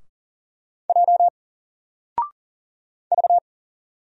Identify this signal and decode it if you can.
Morse code, heard as a keyed tone, about 40 wpm, 700 Hz. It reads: J V
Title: Single Letter-Number Alt 40wpm